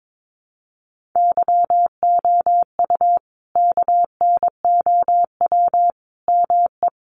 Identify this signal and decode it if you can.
Morse code, heard as a keyed tone, about 22 wpm, 700 Hz. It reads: YOU KNOW ME
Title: Sentences from Top 400 Words 22wpm